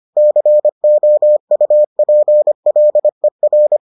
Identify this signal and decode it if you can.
Morse code, heard as a keyed tone, about 25 wpm, 600 Hz. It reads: COUPLER